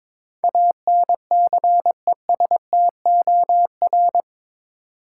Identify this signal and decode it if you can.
Morse code, heard as a keyed tone, about 22 wpm, 700 Hz. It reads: ANCESTOR
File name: practice / 22wpm / Eight Letter Words - 22wpm